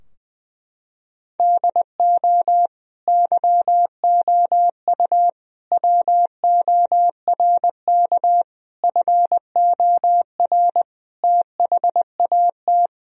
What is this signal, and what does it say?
Morse code, heard as a keyed tone, about 20 wpm, 700 Hz. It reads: DO YOU WORK FOR THAT
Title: Sentences from Top 1000 Words - Review Part 3 - 20wpm